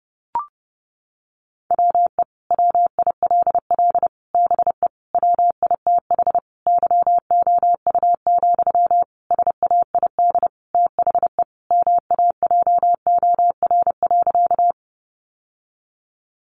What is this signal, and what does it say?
Morse code, heard as a keyed tone, about 30 wpm, 700 Hz. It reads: WE WILL BE WITH YOU, SAID THE MAJOR.
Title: Sentences from Top 600 Words 30wpm